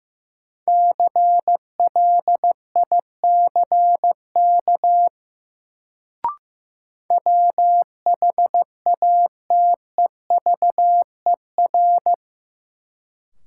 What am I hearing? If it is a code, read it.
Morse code, heard as a keyed tone, about 15 wpm, 700 Hz. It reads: CLICK WHATEVER
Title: Sentences from Top 1100 Words - New Word Drill Alt - 15wpm